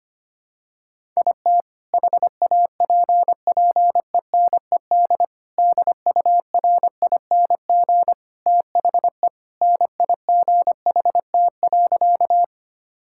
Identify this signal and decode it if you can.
Morse code, heard as a keyed tone, about 25 wpm, 700 Hz. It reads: IT HAPPENED DURING THE NIGHT.
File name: practice / 25wpm / Sentences with 2-Word Phrases - 25wpm